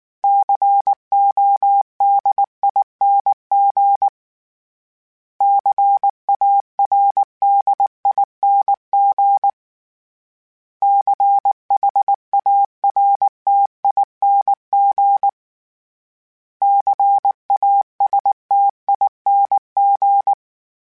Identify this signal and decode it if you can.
Morse code, heard as a keyed tone, about 19 wpm, 800 Hz. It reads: CODING CARDING CHARTING CASTING
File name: Special INGs / Special INGs-mp3s / Sngl Ing @18